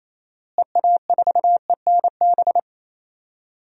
Morse code, heard as a keyed tone, about 28 wpm, 700 Hz. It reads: EA4ENB
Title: DXCC-100 Call Signs - 6 Characters - Repeated 3x - 28wpm